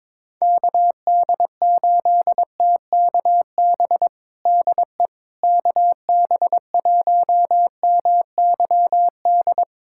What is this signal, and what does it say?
Morse code, heard as a keyed tone, about 22 wpm, 700 Hz. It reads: KD8TKB DE KB1MYD
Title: USA Callsign DE Callsign - 6 Characters - 22wpm